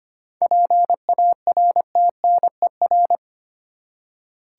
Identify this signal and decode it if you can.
Morse code, heard as a keyed tone, about 25 wpm, 700 Hz. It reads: PARTNER
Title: Seven Letter Words - 25wpm